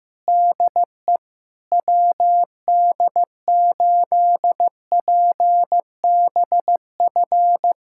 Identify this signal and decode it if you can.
Morse code, heard as a keyed tone, about 16 wpm, 700 Hz. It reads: DE WD8PBF
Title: USA Callsign DE Callsign - 6 Characters - 15wpm